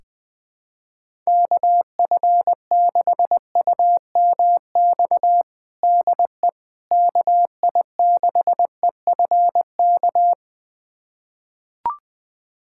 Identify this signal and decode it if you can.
Morse code, heard as a keyed tone, about 21 wpm, 700 Hz. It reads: KF6UMX DE KI6EFK
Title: USA Callsign DE Callsign - 6 Characters - 20wpm